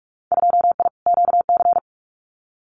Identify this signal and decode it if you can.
Morse code, heard as a keyed tone, about 44 wpm, 700 Hz. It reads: 2I QC
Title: Winter Field Day Exchanges 45wpm